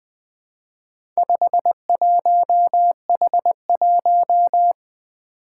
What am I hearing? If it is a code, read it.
Morse code, heard as a keyed tone, about 20 wpm, 700 Hz. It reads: 51H1
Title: Four Challenging Letter-Number Combinations 20wpm